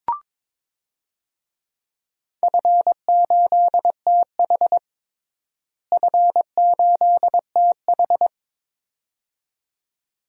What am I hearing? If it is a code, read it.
Morse code, heard as a keyed tone, about 22 wpm, 700 Hz. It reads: F8TH F8TH
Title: DXCC-100 Call Signs - 4 Characters - Repeated 2x - 22wpm